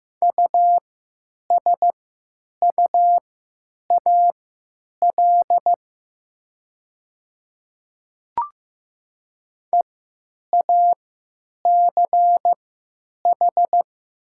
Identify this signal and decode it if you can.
Morse code, heard as a keyed tone, about 15 wpm, spaced as for 9 wpm, 700 Hz. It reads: USUAL EACH
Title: 3x Character Spacing - Top 500 Words - 15wpm